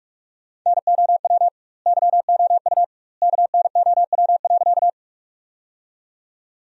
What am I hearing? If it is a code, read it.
Morse code, heard as a keyed tone, about 46 wpm, 700 Hz. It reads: NOW YOU KNOW.
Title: 2x Word-Spacing - Sentences from Top 100 Words - 45wpm